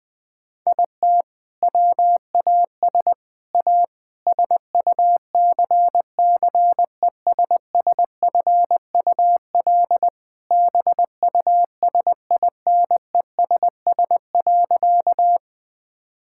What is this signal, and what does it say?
Morse code, heard as a keyed tone, about 20 wpm, 700 Hz. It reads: IT WAS A SUCCESSFUL BUSINESS.
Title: Sentences from Top 1000 Words - Review Part 6 - 20wpm